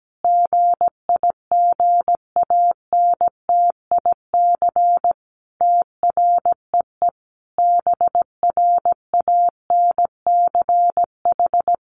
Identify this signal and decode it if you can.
Morse code, heard as a keyed tone, about 17 wpm, 700 Hz. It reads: GIGANTIC TREE BRANCH
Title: A Sound Of Thunder 17wpm